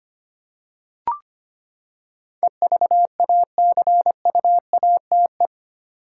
Morse code, heard as a keyed tone, about 25 wpm, 700 Hz. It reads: EVACUATE
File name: Eight Letter Words - 25wpm